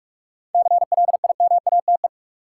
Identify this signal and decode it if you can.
Morse code, heard as a keyed tone, about 42 wpm, 700 Hz. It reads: CLIMATE
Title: Seven Letter Words - 45wpm